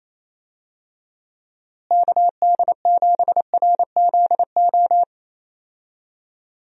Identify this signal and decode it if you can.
Morse code, heard as a keyed tone, about 28 wpm, 700 Hz. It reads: KD7RZO
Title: USA Call Signs - 6 Characters - Repeated 3x - 28wpm